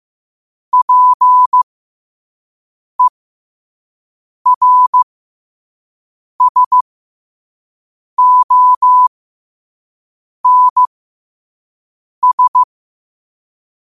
Morse code, heard as a keyed tone, about 15 wpm, spaced as for 5 wpm, 1000 Hz. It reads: PERSONS
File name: 2026-02-15-5wpm